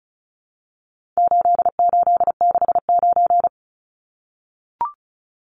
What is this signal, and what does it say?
Morse code, heard as a keyed tone, about 34 wpm, 700 Hz. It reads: 8869
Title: Numbers - 4-Digit 35wpm